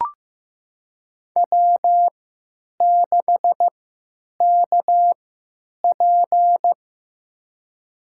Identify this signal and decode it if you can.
Morse code, heard as a keyed tone, about 15 wpm, spaced as for 9 wpm, 700 Hz. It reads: W6KP
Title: USA Call Signs - 4 Characters - 3x Character Spacing - 15wpm